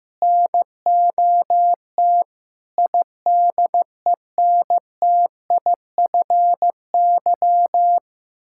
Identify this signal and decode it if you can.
Morse code, heard as a keyed tone, about 14 wpm, 700 Hz. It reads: NOT IDENTIFY
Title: Sentences from Top 1000 Words - Review Part 6 - 15wpm